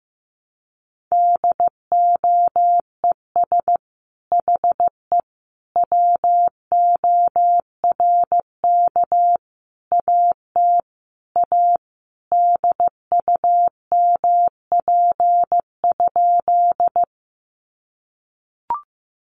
Morse code, heard as a keyed tone, about 15 wpm, 700 Hz. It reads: DOES HE WORK AT A DUMP?
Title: Sentences from Top 1300 Words 15wpm